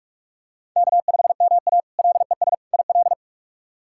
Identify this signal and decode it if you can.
Morse code, heard as a keyed tone, about 46 wpm, 700 Hz. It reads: K5MA LES IL
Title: K1USN SST QSO Practice - 45wpm